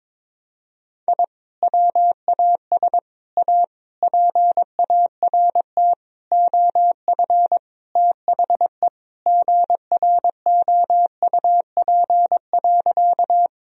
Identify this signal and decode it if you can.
Morse code, heard as a keyed tone, about 22 wpm, 700 Hz. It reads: I WAS A PART OF THE GROUP.
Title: Sentences from Top 1000 Words - Review Part 1 - 22wpm